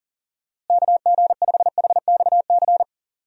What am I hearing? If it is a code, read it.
Morse code, heard as a keyed tone, about 37 wpm, 700 Hz. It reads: KG5HXC
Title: USA Call Signs - 6 Characters - Repeated 3x - 40wpm